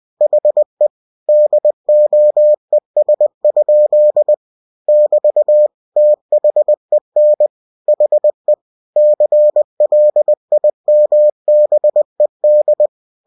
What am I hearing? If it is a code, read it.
Morse code, heard as a keyed tone, about 20 wpm, 600 Hz. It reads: HE DOES? <BT> THEN HE CLIMBED
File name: cw-books / winnie-the-pooh / winnie-20wpm-page-0003